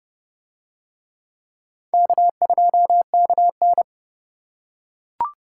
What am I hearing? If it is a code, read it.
Morse code, heard as a keyed tone, about 29 wpm, 700 Hz. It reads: K2KN